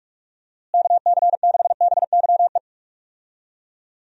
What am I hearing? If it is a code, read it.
Morse code, heard as a keyed tone, about 46 wpm, 700 Hz. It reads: KC6BYE
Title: USA Call Signs - 6 Characters - Repeated 3x - 45wpm